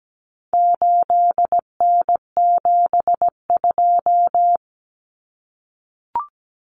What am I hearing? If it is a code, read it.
Morse code, heard as a keyed tone, about 17 wpm, 700 Hz. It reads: 8N72
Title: Common Cut Numbers - 4-Digit 17wpm